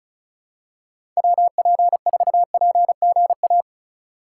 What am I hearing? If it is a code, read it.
Morse code, heard as a keyed tone, about 34 wpm, 700 Hz. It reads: WP4PGA
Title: USA Call Signs - 6 Characters - 4x Character Spacing - 35wpm